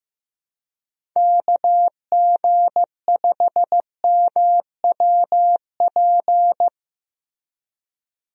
Morse code, heard as a keyed tone, about 15 wpm, 700 Hz. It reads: KG5MWP
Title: USA Call Signs - 6 Characters - Repeated 3x - 15wpm